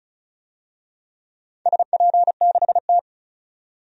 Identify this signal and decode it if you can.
Morse code, heard as a keyed tone, about 36 wpm, 700 Hz. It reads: SP6T
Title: DXCC-100 Call Signs - 4 Characters - 35wpm